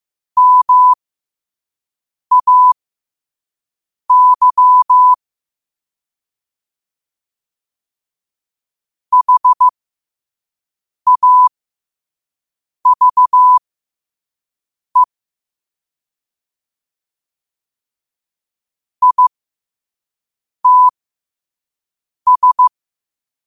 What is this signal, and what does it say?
Morse code, heard as a keyed tone, about 15 wpm, spaced as for 5 wpm, 1000 Hz. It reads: MAY HAVE ITS
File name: 5wpm